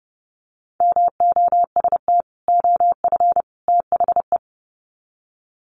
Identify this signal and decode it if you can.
Morse code, heard as a keyed tone, about 30 wpm, 700 Hz. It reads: MOST OF THE
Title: Sets of 3 Words 30wpm